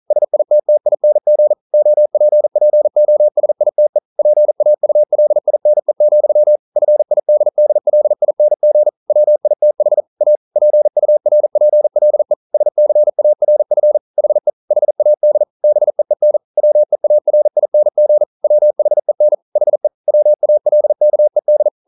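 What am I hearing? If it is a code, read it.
Morse code, heard as a keyed tone, about 40 wpm, 600 Hz. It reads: SITTING OPPOSITE PAULINE, FIDDLING WITH A PURPLE SCARF HE HAD BEEN WEARING WHEN HE WALKED
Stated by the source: Morse code practice at 41 wpm, from CW med Gnister.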